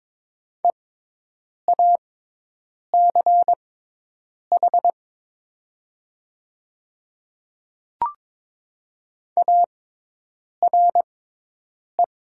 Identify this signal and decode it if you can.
Morse code, heard as a keyed tone, about 22 wpm, spaced as for 8 wpm, 700 Hz. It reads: EACH ARE
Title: 6x Character Spacing - Top 100 Words - 22wpm